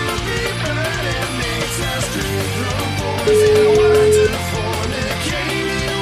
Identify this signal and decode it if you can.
Ringback tone (Europe, 425 Hz), with noise.